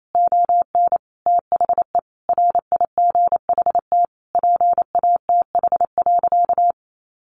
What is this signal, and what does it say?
Morse code, heard as a keyed tone, about 28 wpm, 700 Hz. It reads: ON THE RIGHT PATH.
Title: Sentences from Top 1400 Words 28wpm